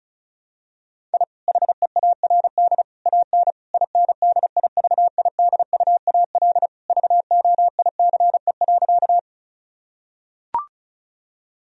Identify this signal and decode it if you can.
Morse code, heard as a keyed tone, about 36 wpm, 700 Hz. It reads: I HEARD AN INDIVIDUAL VOICE.
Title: Sentences from Top 1000 Words - Review Part 4 - 35wpm